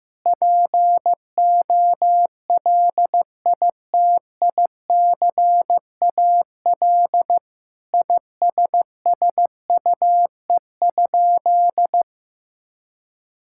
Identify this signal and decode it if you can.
Morse code, heard as a keyed tone, about 15 wpm, 700 Hz. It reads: POLITICAL ISSUE?
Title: Sentences from Top 1000 Words - Review Part 6 - 15wpm